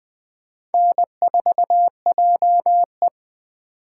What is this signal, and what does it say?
Morse code, heard as a keyed tone, about 25 wpm, 700 Hz. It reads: N4JE
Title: Speed Racing - USA 1x2 Call Signs 20wpm